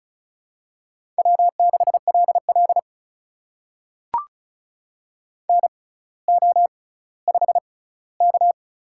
Morse code, heard as a keyed tone, about 36 wpm, spaced as for 18 wpm, 700 Hz. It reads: W6LL NO5K
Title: USA Call Signs - 4 Characters - 6x Character Spacing - 35wpm